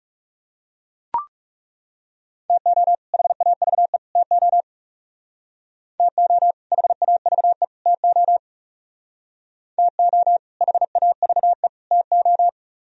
Morse code, heard as a keyed tone, about 40 wpm, 700 Hz. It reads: TO HAVE TO TO HAVE TO TO HAVE TO